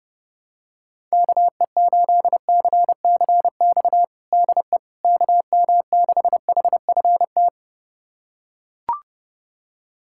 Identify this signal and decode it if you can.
Morse code, heard as a keyed tone, about 30 wpm, 700 Hz. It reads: KE8CCX DE KM6HFT
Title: USA Callsign DE Callsign - 6 Characters - 30wpm